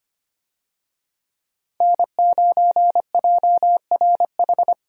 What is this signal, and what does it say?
Morse code, heard as a keyed tone, about 25 wpm, 700 Hz. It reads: N9JRH
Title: USA Call Signs - 5 Characters - Repeated 2x - 25wpm